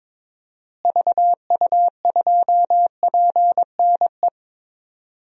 Morse code, heard as a keyed tone, about 22 wpm, 700 Hz. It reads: VU2PNE
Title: DXCC-100 Call Signs - 6 Characters - Repeated 2x - 22wpm